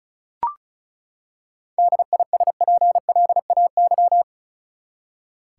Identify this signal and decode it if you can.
Morse code, heard as a keyed tone, about 34 wpm, 700 Hz. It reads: DISPLAY
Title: Seven Letter Words - 35wpm